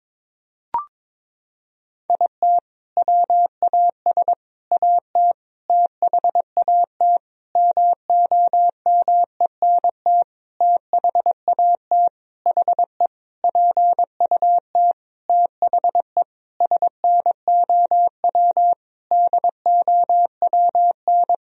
Morse code, heard as a keyed tone, about 22 wpm, 700 Hz. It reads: IT WAS AT THAT MOMENT THAT HE PUT THE SNOW DOWN
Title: Sentences from Top 1000 Words - Review Part 3 - 22wpm